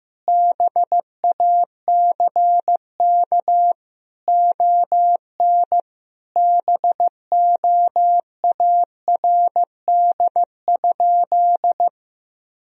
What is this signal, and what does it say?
Morse code, heard as a keyed tone, about 15 wpm, 700 Hz. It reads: BACK ON BOARD?
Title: Sentences from Top 1000 Words - Review Part 4 - 15wpm